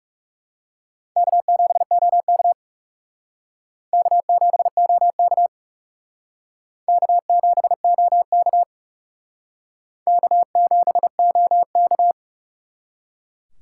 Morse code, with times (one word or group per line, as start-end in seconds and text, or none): 1.2-2.5 K7OK
3.9-5.5 K7OK
6.9-8.6 K7OK
10.1-12.1 K7OK